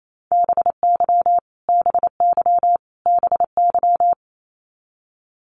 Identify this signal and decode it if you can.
Morse code, heard as a keyed tone, about 28 wpm, 700 Hz. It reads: BY BY BY
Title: Top 500 Words Repeated 28wpm